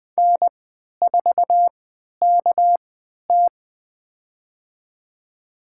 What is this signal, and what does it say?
Morse code, heard as a keyed tone, about 21 wpm, spaced as for 11 wpm, 700 Hz. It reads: N4KT
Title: USA Call Signs - 3x Character Spacing - 20wpm